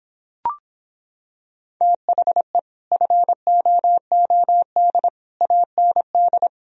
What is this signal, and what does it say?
Morse code, heard as a keyed tone, about 26 wpm, 700 Hz. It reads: THE FOOD AND